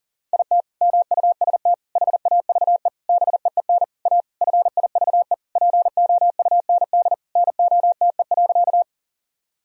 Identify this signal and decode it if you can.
Morse code, heard as a keyed tone, about 40 wpm, 700 Hz. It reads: IT MUST HAVE BEEN A FIVE POUND NOTE.
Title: Sentences from Top 1000 Words - Review Part 2 - 40wpm